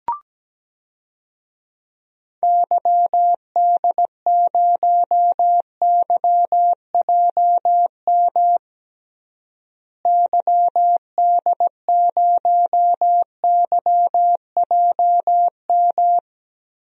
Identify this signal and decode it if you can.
Morse code, heard as a keyed tone, about 17 wpm, 700 Hz. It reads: YD0YJM YD0YJM
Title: DXCC-100 Call Signs - 6 Characters - Repeated 3x - 17wpm